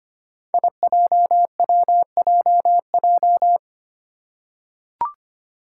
Morse code, heard as a keyed tone, about 25 wpm, 700 Hz. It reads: IJWJJ
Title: Five Challenging Letter-Number Combinations 25wpm